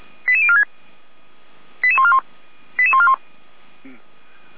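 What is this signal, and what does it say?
Five-tone selective call (ZVEI), heard as quick runs of five tones, tones of about 70 ms. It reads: 88046 88141 88141
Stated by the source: ZVEI five-tone selective call received on an SDR, logged by the Feuerwehr Orlamünde fire brigade server.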